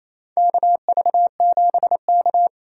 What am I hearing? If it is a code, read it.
Morse code, heard as a keyed tone, about 27 wpm, 700 Hz. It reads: KV7K
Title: USA Call Signs - 4 Characters - 2x Character Spacing - 28wpm